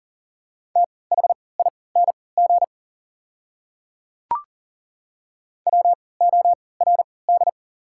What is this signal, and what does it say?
Morse code, heard as a keyed tone, about 40 wpm, spaced as for 23 wpm, 690 Hz. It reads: THING WORD
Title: 3x Character Spacing - Top 100 Words - 40wpm